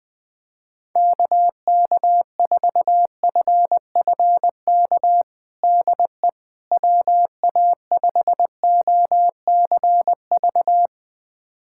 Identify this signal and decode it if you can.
Morse code, heard as a keyed tone, about 20 wpm, 700 Hz. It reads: KK4FFK DE WA5OCV
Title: USA Callsign DE Callsign - 6 Characters - 20wpm